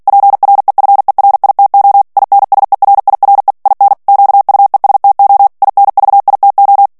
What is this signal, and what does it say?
Morse code, heard as a keyed tone, about 48 wpm, 800 Hz. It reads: PREFERITO INSERIRE IN QUESTO INVITO
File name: morse mp3 / ITALIEN texte clair 1 vitesse 45 mots minute